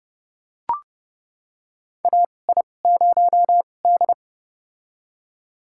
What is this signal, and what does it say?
Morse code, heard as a keyed tone, about 30 wpm, spaced as for 22 wpm, 700 Hz. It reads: AI0D
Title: USA Call Signs - 4 Characters - 2x Character Spacing - 30wpm